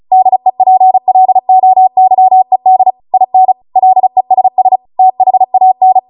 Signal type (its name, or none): Morse code